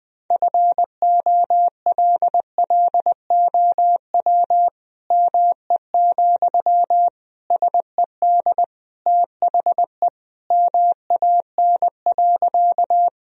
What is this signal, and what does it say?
Morse code, heard as a keyed tone, about 21 wpm, 700 Hz. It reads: FOLLOW ME, SED THE MAN.
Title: Sentences from Top 200 Words and Abbreviations 20wpm